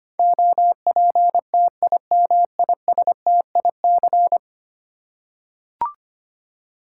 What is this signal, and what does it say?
Morse code, heard as a keyed tone, about 25 wpm, 700 Hz. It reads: OPTIMISTIC
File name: Ten Letter Words - 25wpm